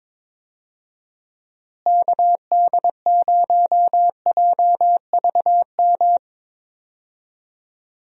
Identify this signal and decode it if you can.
Morse code, heard as a keyed tone, about 22 wpm, 700 Hz. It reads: KD0JVM
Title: USA Call Signs - 6 Characters - Repeated 2x - 22wpm